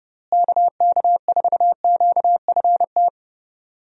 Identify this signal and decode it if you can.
Morse code, heard as a keyed tone, about 30 wpm, 700 Hz. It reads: KK4QFT
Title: USA Call Signs - 6 Characters - Repeated 3x - 30wpm